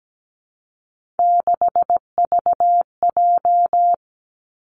Morse code, heard as a keyed tone, about 17 wpm, 700 Hz. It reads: 6VJ
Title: Three Challenging Letter-Number Combinations 17wpm